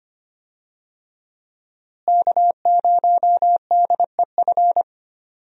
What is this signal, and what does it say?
Morse code, heard as a keyed tone, about 25 wpm, 700 Hz. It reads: K0DEF
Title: USA Call Signs - 5 Characters - Repeated 2x - 25wpm